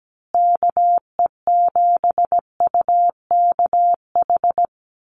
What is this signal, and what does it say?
Morse code, heard as a keyed tone, about 17 wpm, 700 Hz. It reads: KE7UKH
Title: USA Call Signs Repeated 3x 17wpm